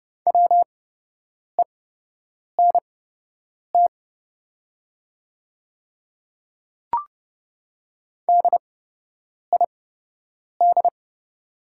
Morse code, heard as a keyed tone, about 30 wpm, spaced as for 8 wpm, 700 Hz. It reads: WENT DID
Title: 8x Character Spacing - Top 300 Words - 30wpm